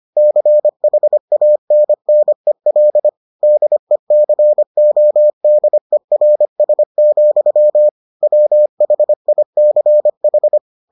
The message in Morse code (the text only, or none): CHANNEL DECODERS, WHICH